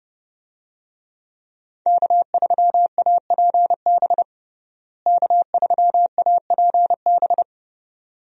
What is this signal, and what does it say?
Morse code, heard as a keyed tone, about 30 wpm, 700 Hz. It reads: K3APB K3APB
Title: USA Call Signs - 5 Characters - Repeated 2x - 30wpm